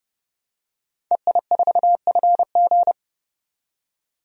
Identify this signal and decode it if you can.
Morse code, heard as a keyed tone, about 32 wpm, 700 Hz. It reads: EI4FG
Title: DXCC-100 Call Signs - 5 Characters - 30wpm